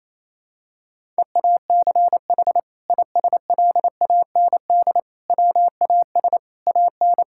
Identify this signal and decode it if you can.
Morse code, heard as a keyed tone, about 28 wpm, 700 Hz. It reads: EACH ISLAND WAS AN
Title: Sentences from Top 1000 Words - Review Part 6 - 28wpm